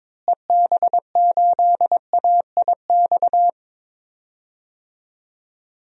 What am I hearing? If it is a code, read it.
Morse code, heard as a keyed tone, about 22 wpm, 700 Hz. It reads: EB8AIX
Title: DXCC-100 Call Signs - 6 Characters - Repeated 3x - 22wpm